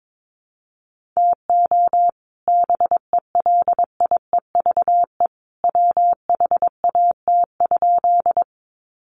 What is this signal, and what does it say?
Morse code, heard as a keyed tone, about 22 wpm, 700 Hz. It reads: TO BELIEVE WHAT?
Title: Sentences from Top 1000 Words 22wpm